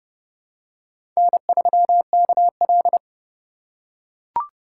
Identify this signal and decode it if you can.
Morse code, heard as a keyed tone, about 32 wpm, 700 Hz. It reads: N3KL
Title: USA Call Signs - 3x Character Spacing - 30wpm